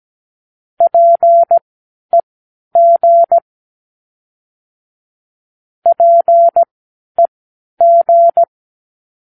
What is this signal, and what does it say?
Morse code, heard as a keyed tone, about 17 wpm, spaced as for 11 wpm, 680 Hz. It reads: PEG PEG
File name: k7qo / Practicewords